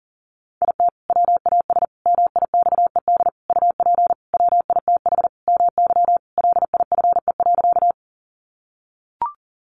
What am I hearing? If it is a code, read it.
Morse code, heard as a keyed tone, about 40 wpm, 700 Hz. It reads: IT WAS MIXED UP WITH MY LIFE.
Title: Sentences from Top 1500 Words 40wpm